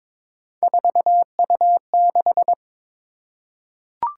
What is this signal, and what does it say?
Morse code, heard as a keyed tone, about 22 wpm, 700 Hz. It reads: 4U6
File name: practice / 22wpm / Three Letter-Number Combinations - 22wpm